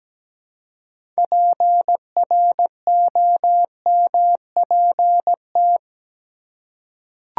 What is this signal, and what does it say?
Morse code, heard as a keyed tone, about 17 wpm, 700 Hz. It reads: PROMPT
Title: Six Letter Words - 17wpm